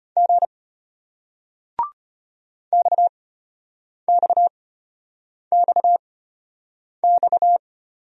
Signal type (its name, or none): Morse code